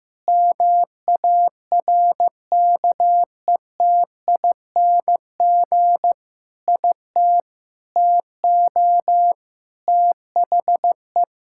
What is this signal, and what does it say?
Morse code, heard as a keyed tone, about 15 wpm, 700 Hz. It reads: MARKETING IT TO THE
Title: Sentences from Top 1000 Words - Review Part 5 - 15wpm